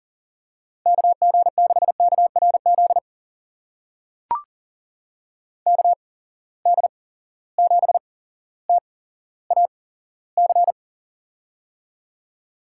Morse code, heard as a keyed tone, about 39 wpm, spaced as for 17 wpm, 700 Hz. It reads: KG6KRZ KD7TAC
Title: USA Call Signs - 6 Characters - 8x Character Spacing - 40wpm